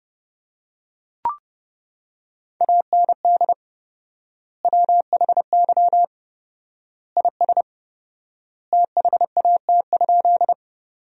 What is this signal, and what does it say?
Morse code, heard as a keyed tone, about 30 wpm, 700 Hz. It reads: AND WHY IS THAT?
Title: 4x Word-Spacing - Sentences from Top 200 Words - 30wpm